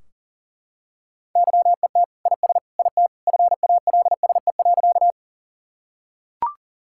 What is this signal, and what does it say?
Morse code, heard as a keyed tone, about 40 wpm, 700 Hz. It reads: YET IS IT FALSE.
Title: Sentences from Top 1600 Words 40wpm